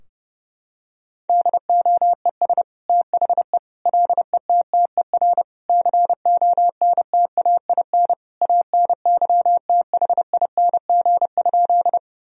Morse code, heard as a keyed tone, about 30 wpm, 700 Hz. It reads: DOES THE LETTER CONTAIN ANYTHING?
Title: Sentences from Top 1000 Words - Review Part 2 - 30wpm